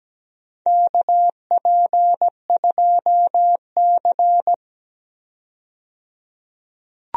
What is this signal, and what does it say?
Morse code, heard as a keyed tone, about 17 wpm, 700 Hz. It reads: KP2C
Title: DXCC-100 Call Signs - 4 Characters - 17wpm